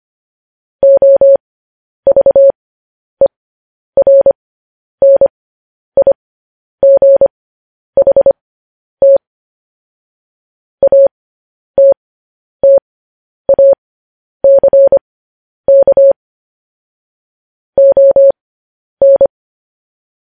Morse code, heard as a keyed tone, about 25 wpm, spaced as for 10 wpm, 550 Hz. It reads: OVERNIGHT ATTACK ON